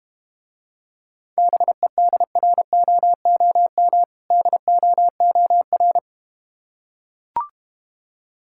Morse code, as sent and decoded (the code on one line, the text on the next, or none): -... -.. .-. --- --- -- / -.. --- --- .-.
BEDROOM DOOR